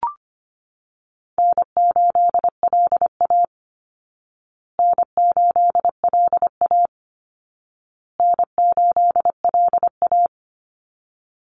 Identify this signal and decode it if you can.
Morse code, heard as a keyed tone, about 25 wpm, 700 Hz. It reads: N8LA N8LA N8LA
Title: Easy Call Signs Repeated 3x - 25wpm